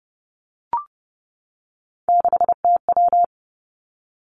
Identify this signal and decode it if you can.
Morse code, heard as a keyed tone, about 31 wpm, 700 Hz. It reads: 6TW
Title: Three Challenging Letter-Number Combinations 30wpm